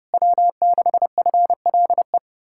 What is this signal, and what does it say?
Morse code, heard as a keyed tone, about 34 wpm, 700 Hz. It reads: W6FLE
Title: Speed-Racing Common QSO Phrases Encore - 30wpm